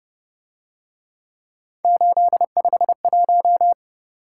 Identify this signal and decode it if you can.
Morse code, heard as a keyed tone, about 29 wpm, 700 Hz. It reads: 851